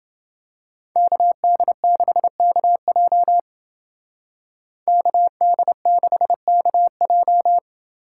Morse code, heard as a keyed tone, about 28 wpm, 700 Hz. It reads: KD6KJ KD6KJ